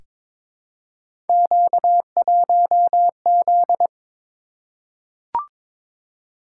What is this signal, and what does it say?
Morse code, heard as a keyed tone, about 22 wpm, 700 Hz. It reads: Q1Z
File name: practice / 22wpm / Three Letter-Number Combinations - 22wpm